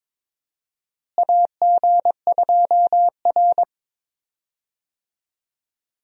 Morse code, heard as a keyed tone, about 21 wpm, 700 Hz. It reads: AG2R
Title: USA Call Signs - 4 Characters - Repeated 3x - 22wpm